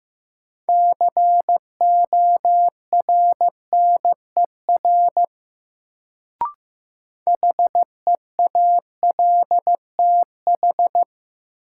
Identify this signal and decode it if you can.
Morse code, heard as a keyed tone, about 16 wpm, 700 Hz. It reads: CORNER HEALTH
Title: Sets of 5 Words - New Word Drill - 15wpm